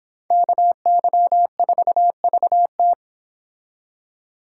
Morse code, heard as a keyed tone, about 26 wpm, 700 Hz. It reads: KY4VT